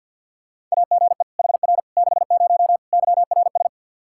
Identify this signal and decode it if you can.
Morse code, heard as a keyed tone, about 52 wpm, 700 Hz. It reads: AGE HR 60 YRS
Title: Practice QSO - Exchange 3 - More Info - 50wpm